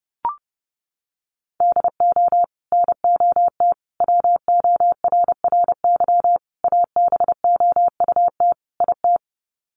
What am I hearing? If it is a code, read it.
Morse code, heard as a keyed tone, about 28 wpm, 700 Hz. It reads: DO NOT WORRY ABOUT IT
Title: Sets of 5 Words 30wpm